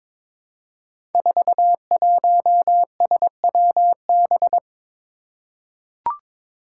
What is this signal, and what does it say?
Morse code, heard as a keyed tone, about 22 wpm, 700 Hz. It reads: 41SWB
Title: Five Challenging Letter-Number Combinations 22wpm